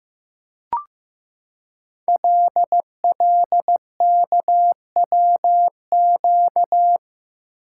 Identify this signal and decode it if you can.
Morse code, heard as a keyed tone, about 15 wpm, 700 Hz. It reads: LLKWQ
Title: Code Groups - 5 Letter:Numbers - 15wpm